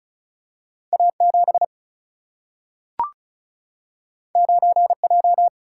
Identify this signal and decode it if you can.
Morse code, heard as a keyed tone, about 36 wpm, 700 Hz. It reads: A 7 9 J
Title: Single-Letter Single-Number 35wpm